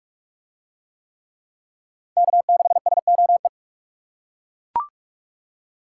Morse code, heard as a keyed tone, about 41 wpm, 700 Hz. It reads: K6SOE